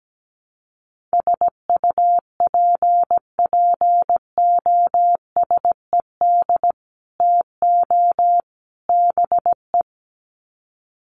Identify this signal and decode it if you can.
Morse code, heard as a keyed tone, about 17 wpm, 700 Hz. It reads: SUPPOSED TO BE
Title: Sets of 3 Words 17wpm